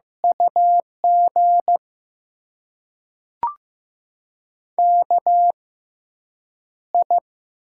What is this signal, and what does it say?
Morse code, heard as a keyed tone, about 16 wpm, spaced as for 8 wpm, 700 Hz. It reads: UG KI
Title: USA Call Signs - 5 Characters - 6x Character Spacing - 15wpm